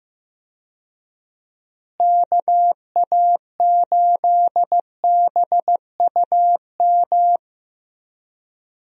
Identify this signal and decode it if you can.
Morse code, heard as a keyed tone, about 15 wpm, 700 Hz. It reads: KA8BUM
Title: USA Call Signs - 6 Characters - Repeated 3x - 15wpm